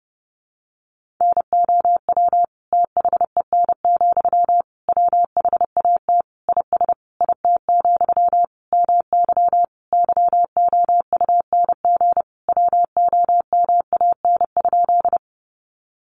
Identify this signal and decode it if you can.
Morse code, heard as a keyed tone, about 30 wpm, 700 Hz. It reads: NOW THEN, WHAT IS IT, MY YOUNG WOMAN?
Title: Sentences from Top 1100 Words 30wpm